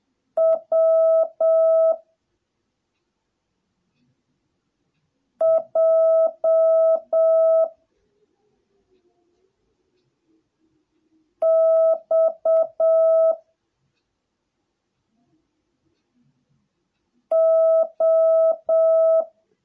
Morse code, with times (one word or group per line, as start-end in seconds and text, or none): 0.4-1.9 W
5.4-7.6 J
11.4-13.3 X
17.3-19.2 O